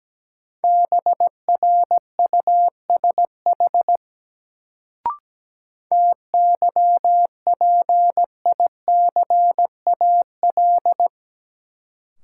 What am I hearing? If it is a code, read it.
Morse code, heard as a keyed tone, about 17 wpm, 700 Hz. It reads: BRUSH TYPICAL
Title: Sentences from Top 1100 Words - New Word Drill - 17wpm